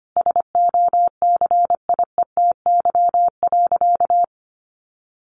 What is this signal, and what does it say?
Morse code, heard as a keyed tone, about 25 wpm, 700 Hz. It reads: SOCIETY.
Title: Sentences from Top 1500 Words 25wpm